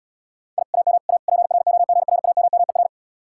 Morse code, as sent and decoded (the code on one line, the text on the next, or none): .. / .- -- / .- / .--. .-. --- --. .-. .- -- -- . .-.
I AM A PROGRAMMER